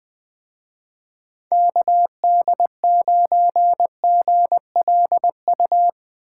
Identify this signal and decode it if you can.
Morse code, heard as a keyed tone, about 20 wpm, 700 Hz. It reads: KD9GLU
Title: Speed Racing - USA 2x3 Call Signs 17wpm